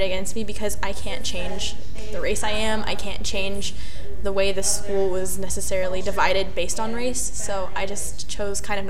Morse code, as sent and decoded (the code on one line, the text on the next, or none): none